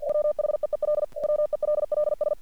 Morse code, heard as a keyed tone, about 48 wpm, 600 Hz. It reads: WHEEL WELLS
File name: archived / CRSnet